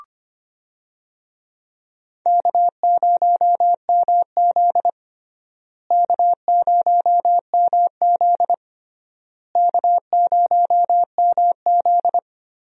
Morse code, heard as a keyed tone, about 25 wpm, 700 Hz. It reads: K0MZ K0MZ K0MZ
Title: USA Call Signs - 4 Characters - Repeated 3x - 25wpm